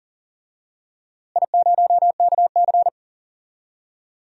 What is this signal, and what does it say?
Morse code, heard as a keyed tone, about 39 wpm, 700 Hz. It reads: I0KC
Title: DXCC-100 Call Signs - 4 Characters - Repeated 2x - 40wpm